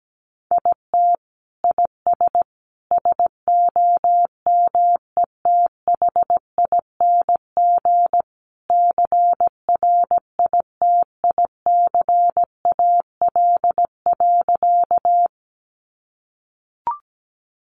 Morse code, heard as a keyed tone, about 17 wpm, 700 Hz. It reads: IT IS SOMETHING CRITICAL.
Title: Sentences from Top 1100 Words 17wpm